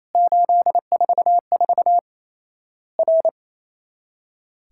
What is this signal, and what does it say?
Morse code, heard as a keyed tone, about 28 wpm, 700 Hz. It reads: 844 R
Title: Worked All Europe DX Contest - QTC Exchange - 28wpm